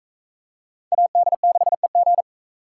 Morse code, heard as a keyed tone, about 37 wpm, 700 Hz. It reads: AD6EG